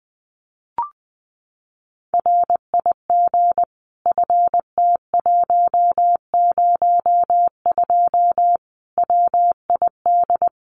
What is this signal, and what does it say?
Morse code, heard as a keyed tone, about 20 wpm, 700 Hz. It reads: RIG FT102 WID
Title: Common QSO Phrases 20wpm